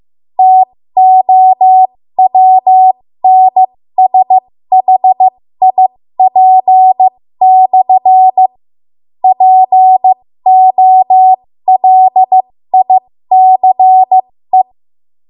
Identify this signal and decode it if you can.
Morse code, heard as a keyed tone, about 15 wpm, spaced as for 13 wpm, 750 Hz. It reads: TOWNSHIP/ POLICE
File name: Morse / 13WPM